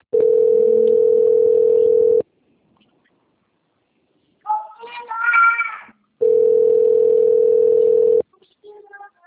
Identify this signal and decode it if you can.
Ringback tone (North America), 440 + 480 Hz.